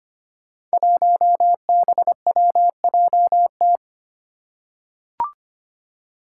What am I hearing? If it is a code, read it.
Morse code, heard as a keyed tone, about 25 wpm, 700 Hz. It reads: 1BWJT
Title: Five Challenging Letter-Number Combinations 25wpm